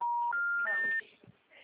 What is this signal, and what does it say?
Special information tone, with noise.
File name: out- -2002- - - .51